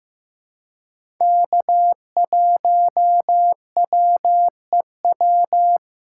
Morse code, heard as a keyed tone, about 15 wpm, 700 Hz. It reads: K1WEW
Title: USA Call Signs - 5 Characters - 4x Character Spacing - 15wpm